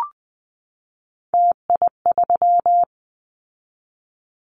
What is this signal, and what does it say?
Morse code, heard as a keyed tone, about 20 wpm, 700 Hz. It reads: TI3
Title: Callsign Prefixes for Top 100 DXCC - 20wpm